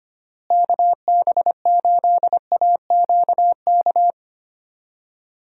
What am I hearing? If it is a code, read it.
Morse code, heard as a keyed tone, about 25 wpm, 700 Hz. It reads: KB8AQK
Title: USA Call Signs - 6 Characters - Repeated 3x - 25wpm